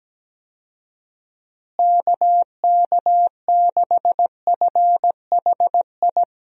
Morse code, heard as a keyed tone, about 17 wpm, 700 Hz. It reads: KK6FHI
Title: USA Call Signs - 6 Characters - Repeated 2x - 17wpm